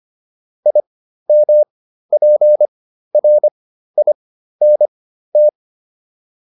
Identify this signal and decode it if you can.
Morse code, heard as a keyed tone, about 25 wpm, spaced as for 13 wpm, 600 Hz. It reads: IMPRINT
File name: IM7 13